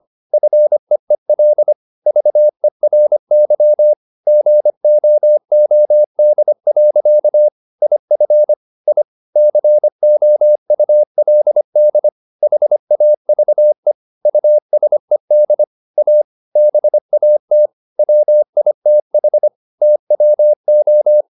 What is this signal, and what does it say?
Morse code, heard as a keyed tone, about 25 wpm, 600 Hz. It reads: FEEL VERY GOOD. IF I COULD HAVE USED A BAT WITH TWO